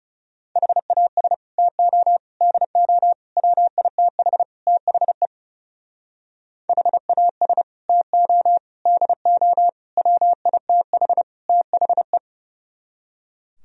Morse code, heard as a keyed tone, about 39 wpm, 700 Hz. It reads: HAS TO DO WITH THE HAS TO DO WITH THE